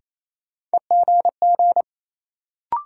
Morse code, heard as a keyed tone, about 28 wpm, 700 Hz. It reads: EGG
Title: Sentences from Top 1000 Words - Review Part 7 - New Word Drill - 28wpm